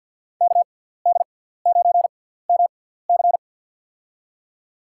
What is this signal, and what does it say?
Morse code, heard as a keyed tone, about 52 wpm, spaced as for 17 wpm, 700 Hz. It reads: KD9MC
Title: USA Call Signs - 5 Characters - 6x Character Spacing - 50wpm